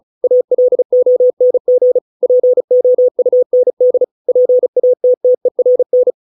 Morse code, heard as a keyed tone, about 34 wpm, 500 Hz. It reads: ALONG POUND PATTERN